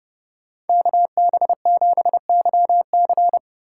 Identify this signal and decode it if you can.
Morse code, heard as a keyed tone, about 29 wpm, 700 Hz. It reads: KB7YC